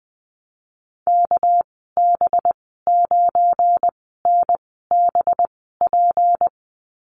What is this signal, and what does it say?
Morse code, heard as a keyed tone, about 19 wpm, spaced as for 14 wpm, 700 Hz. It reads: KB9NBP
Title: 207 Lesson 33g - Letter j - Call Signs 20wpm